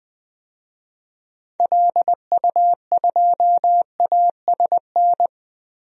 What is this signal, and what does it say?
Morse code, heard as a keyed tone, about 20 wpm, 700 Hz. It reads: LU2ASN
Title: DXCC-100 Call Signs - 6 Characters - 20wpm